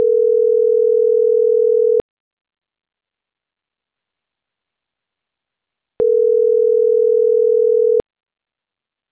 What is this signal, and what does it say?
Ringback tone (North America), 440 + 480 Hz.